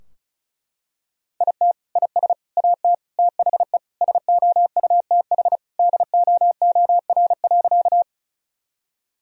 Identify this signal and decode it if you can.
Morse code, heard as a keyed tone, about 34 wpm, 700 Hz. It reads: IT IS AT THE SOUTH DOOR.
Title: Sentences from Top 1000 Words - Review Part 2 - 35wpm